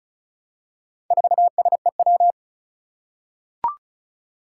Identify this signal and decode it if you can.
Morse code, heard as a keyed tone, about 36 wpm, 700 Hz. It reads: 4SEW
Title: Four Challenging Letter-Number Combinations 35wpm